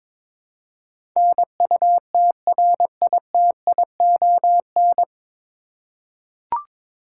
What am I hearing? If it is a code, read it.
Morse code, heard as a keyed tone, about 22 wpm, 700 Hz. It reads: NUTRITION
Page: Morse Code Ninja practice 22wpm